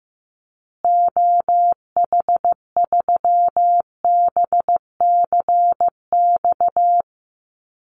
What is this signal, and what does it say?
Morse code, heard as a keyed tone, about 14 wpm, 700 Hz. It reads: OH3BCX
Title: Call Signs Repeated 2x - 15wpm